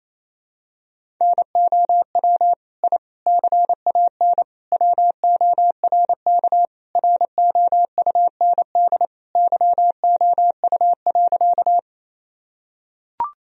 Morse code, heard as a keyed tone, about 28 wpm, 700 Hz. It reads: NOW I CAN WORK ROUND YOU.
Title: Sentences from Top 1000 Words - Review Part 1 - 28wpm